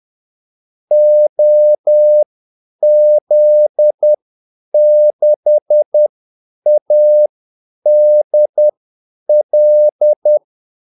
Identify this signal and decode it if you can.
Morse code, heard as a keyed tone, about 10 wpm, spaced as for 8 wpm, 600 Hz. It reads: OZ6ADL
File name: Callsigns-10-8wpm